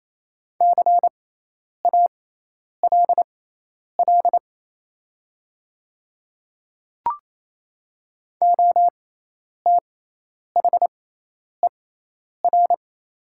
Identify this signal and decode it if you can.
Morse code, heard as a keyed tone, about 28 wpm, spaced as for 10 wpm, 690 Hz. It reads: CALL OTHER
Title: 6x Character Spacing - Top 200 Words - 28wpm